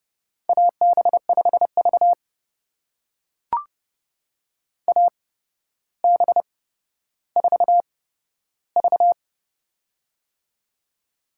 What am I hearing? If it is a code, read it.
Morse code, heard as a keyed tone, about 30 wpm, spaced as for 13 wpm, 700 Hz. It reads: AB5V AB4V
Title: USA Call Signs - 4 Characters - 8x Character Spacing - 30wpm